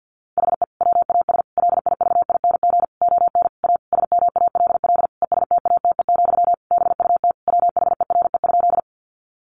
Morse code, heard as a keyed tone, about 49 wpm, 700 Hz. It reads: HE WAS LIVING ON A SMALL ESTATE, BUT WHERE?
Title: Sentences from Top 1200 Words 50wpm